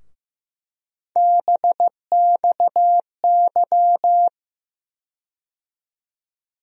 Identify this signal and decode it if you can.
Morse code, heard as a keyed tone, about 15 wpm, 700 Hz. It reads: BXY
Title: USA Call Sign Postfixes - 3 Letters 15wpm